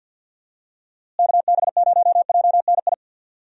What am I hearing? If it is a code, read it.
Morse code, heard as a keyed tone, about 52 wpm, 700 Hz. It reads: KB0JNI
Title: USA Call Signs - 6 Characters - Repeated 2x - 50wpm